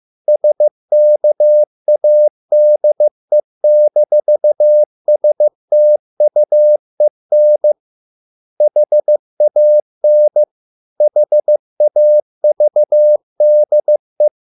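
Morse code, heard as a keyed tone, about 15 wpm, 600 Hz. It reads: SKADE-STUEN HAN HAVDE
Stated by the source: Morse code at a slow 15 wpm from CW med Gnister.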